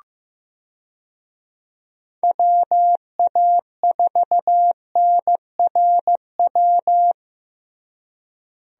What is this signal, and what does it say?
Morse code, heard as a keyed tone, about 15 wpm, 700 Hz. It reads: WA4NRW
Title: USA Call Signs - 6 Characters - Repeated 3x - 15wpm